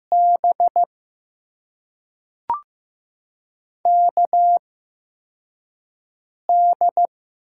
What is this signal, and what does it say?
Morse code, heard as a keyed tone, about 15 wpm, spaced as for 4 wpm, 700 Hz. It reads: B KD
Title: USA Call Signs - 6 Characters - 8x Character Spacing - 15wpm